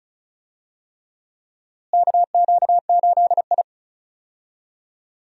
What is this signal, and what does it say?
Morse code, heard as a keyed tone, about 36 wpm, 700 Hz. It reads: KQ8I
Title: USA Call Signs - 4 Characters - Repeated 3x - 35wpm